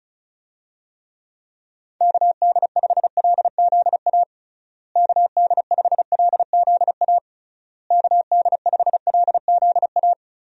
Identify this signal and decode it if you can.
Morse code, heard as a keyed tone, about 34 wpm, 700 Hz. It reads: KD5LZA KD5LZA KD5LZA
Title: USA Call Signs - 6 Characters - Repeated 3x - 35wpm